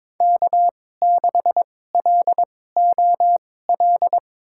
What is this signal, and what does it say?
Morse code, heard as a keyed tone, about 21 wpm, spaced as for 16 wpm, 700 Hz. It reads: K6LOL
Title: USA Call Signs - 5 Characters - 2x Character Spacing - 22wpm